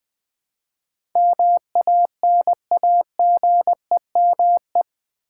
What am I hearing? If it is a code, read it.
Morse code, heard as a keyed tone, about 20 wpm, 700 Hz. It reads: MANAGEME
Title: Build A Word - 4 Letters At A Time - 10-Letter Words - 20wpm